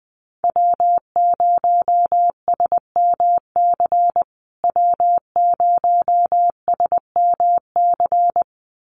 Morse code, heard as a keyed tone, about 20 wpm, 700 Hz. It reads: W0SMC W0SMC
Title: USA Call Signs Repeated 2x 20wpm